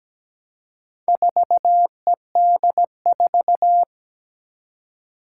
Morse code, heard as a keyed tone, about 17 wpm, 700 Hz. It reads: 4ED4
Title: Four Challenging Letter-Number Combinations 17wpm